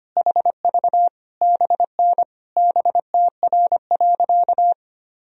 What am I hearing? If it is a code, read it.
Morse code, heard as a keyed tone, about 25 wpm, 700 Hz. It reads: HV BN BTR.
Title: Sentences from Top 500 Words and Abbreviations 25wpm